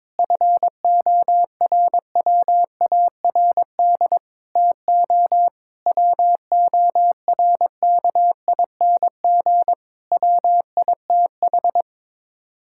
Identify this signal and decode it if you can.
Morse code, heard as a keyed tone, about 22 wpm, 700 Hz. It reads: FORWARD TO WORKING WITH
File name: practice / 22wpm / Sets of 6 Words - 22wpm